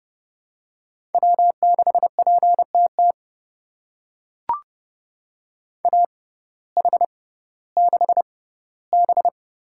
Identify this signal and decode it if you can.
Morse code, heard as a keyed tone, about 30 wpm, spaced as for 17 wpm, 700 Hz. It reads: W6PTT AH6B
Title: USA Call Signs - 5 Characters - 6x Character Spacing - 30wpm